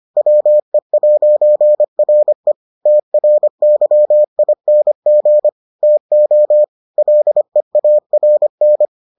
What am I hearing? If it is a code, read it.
Morse code, heard as a keyed tone, about 25 wpm, 600 Hz. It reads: WE'RE TRYING TO LEARN